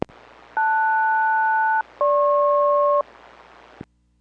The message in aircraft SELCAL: LS-GP